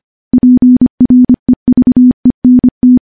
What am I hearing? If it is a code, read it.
Morse code, heard as a keyed tone, about 25 wpm, 250 Hz. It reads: PREVENT
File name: morse / morse-training-25WPM